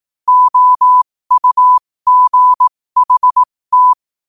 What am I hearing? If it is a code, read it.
Morse code, heard as a keyed tone, about 18 wpm, spaced as for 15 wpm, 1000 Hz. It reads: OUGHT
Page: New quotes every day in morse code at 15 Words per minute.